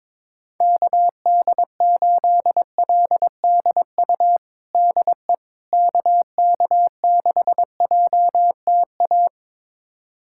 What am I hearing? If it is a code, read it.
Morse code, heard as a keyed tone, about 22 wpm, 700 Hz. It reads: KD8LDU DE KK6JTA
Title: USA Callsign DE Callsign - 6 Characters - 22wpm